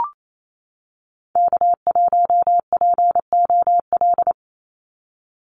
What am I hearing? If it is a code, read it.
Morse code, heard as a keyed tone, about 28 wpm, 700 Hz. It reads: K1POL
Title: USA Call Signs 28wpm